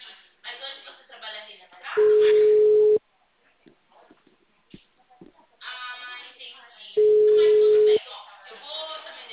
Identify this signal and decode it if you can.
Ringback tone (Europe, 425 Hz), with speech.
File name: out- -2004- - - .12198